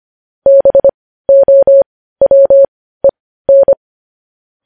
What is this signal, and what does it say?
Morse code, heard as a keyed tone, about 25 wpm, spaced as for 15 wpm, 550 Hz. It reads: BOWEN